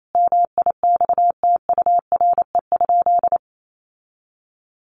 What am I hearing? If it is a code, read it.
Morse code, heard as a keyed tone, about 28 wpm, 700 Hz. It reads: MIXTURE?
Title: Sentences from Top 1500 Words 28wpm